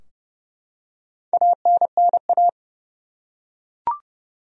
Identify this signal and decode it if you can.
Morse code, heard as a keyed tone, about 30 wpm, 700 Hz. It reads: ANNA
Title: USA Common Names - Max 4 Letters - 30wpm